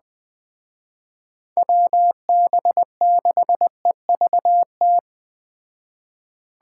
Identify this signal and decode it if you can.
Morse code, heard as a keyed tone, about 20 wpm, 700 Hz. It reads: WB6EVT
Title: Speed Racing - USA 2x3 Call Signs 17wpm